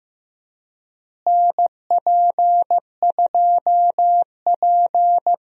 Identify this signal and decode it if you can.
Morse code, heard as a keyed tone, about 15 wpm, 700 Hz. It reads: NP2P
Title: USA Call Signs - 4 Characters - Repeated 3x - 15wpm